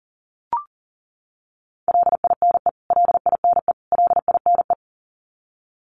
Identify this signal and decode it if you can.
Morse code, heard as a keyed tone, about 40 wpm, 700 Hz. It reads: LINE LINE LINE
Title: Top 500 Words Repeated 40wpm